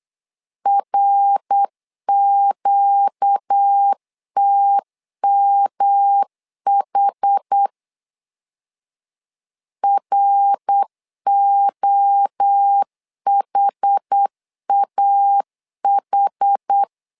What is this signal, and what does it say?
Morse code, heard as a keyed tone, about 8 wpm, 800 Hz. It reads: RQTMH ROHAH
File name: 4 Q 8wpm